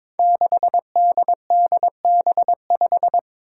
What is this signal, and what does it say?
Morse code, heard as a keyed tone, about 22 wpm, 700 Hz. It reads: 6DDB5
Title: Five Challenging Letter-Number Combinations 22wpm